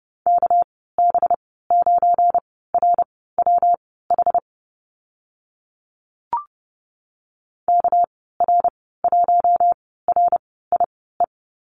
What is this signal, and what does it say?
Morse code, heard as a keyed tone, about 31 wpm, spaced as for 17 wpm, 700 Hz. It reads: KB9RWH KR1RIE
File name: Lesson 37g - Letter x - Call Signs 30wpm